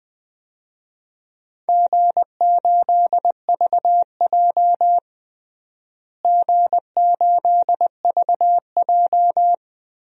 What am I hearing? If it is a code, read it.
Morse code, heard as a keyed tone, about 20 wpm, 700 Hz. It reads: G8VJ G8VJ
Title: DXCC-100 Call Signs - 4 Characters - Repeated 3x - 20wpm